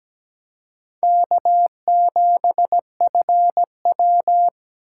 Morse code, heard as a keyed tone, about 17 wpm, 700 Hz. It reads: K7FW
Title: USA Call Signs - 4 Characters - 6x Character Spacing - 17wpm